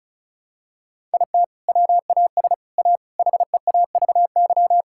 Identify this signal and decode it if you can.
Morse code, heard as a keyed tone, about 36 wpm, 700 Hz. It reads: IT WAS A HEAVY
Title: Sentences from Top 1600 Words 35wpm